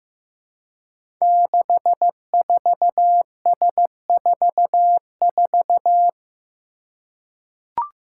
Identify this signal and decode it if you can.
Morse code, heard as a keyed tone, about 15 wpm, 700 Hz. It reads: 64S44
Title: Five Challenging Letter-Number Combinations 15wpm